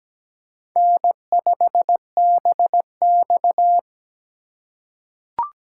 Morse code, heard as a keyed tone, about 17 wpm, 700 Hz. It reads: N5BX
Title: USA Call Signs - 4 Characters - 3x Character Spacing - 17wpm